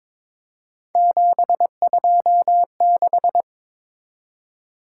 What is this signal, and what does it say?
Morse code, heard as a keyed tone, about 22 wpm, 700 Hz. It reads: 726
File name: practice / 22wpm / Three Letter-Number Combinations - 22wpm